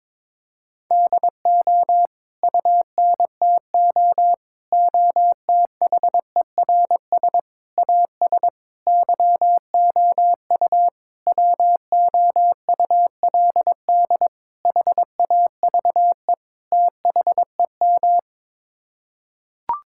Morse code, heard as a keyed tone, about 22 wpm, 700 Hz. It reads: DO UNTO OTHERS AS YOU WOULD HAVE THEM
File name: practice / 22wpm / Sets of 8 Words - 22wpm